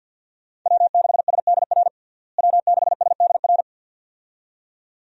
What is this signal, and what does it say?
Morse code, heard as a keyed tone, about 52 wpm, 700 Hz. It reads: W6SDR W6SDR
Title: USA Call Signs - 5 Characters - Repeated 2x - 50wpm